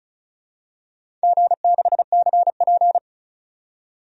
Morse code, heard as a keyed tone, about 35 wpm, 700 Hz. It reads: G6CP
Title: DXCC-100 Call Signs - 4 Characters - 35wpm